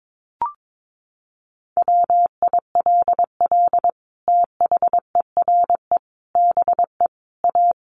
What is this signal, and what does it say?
Morse code, heard as a keyed tone, about 22 wpm, 700 Hz. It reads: WILL THERE BE A
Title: Sentences from Top 1300 Words 22wpm